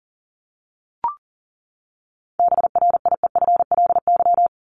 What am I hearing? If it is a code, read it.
Morse code, heard as a keyed tone, about 40 wpm, 700 Hz. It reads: BRIEFLY
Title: LY Suffix - Top 100 Words with Base Words 40wpm